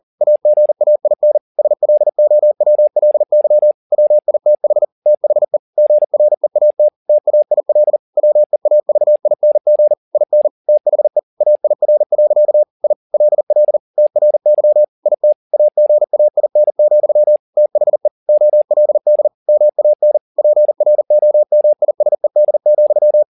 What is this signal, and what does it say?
Morse code, heard as a keyed tone, about 40 wpm, 600 Hz. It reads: AGAIN SLOWLY WITH THE GREAT TAIL WEAVING IN THE AIR. I LL TRY IT AGAIN, THE OLD MAN PROMISED,
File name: the old man and the sea 2 40wpm